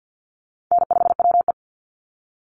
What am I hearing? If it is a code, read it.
Morse code, heard as a keyed tone, about 50 wpm, 700 Hz. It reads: N5WE
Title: Easy Call Signs Repeated 3x - 50wpm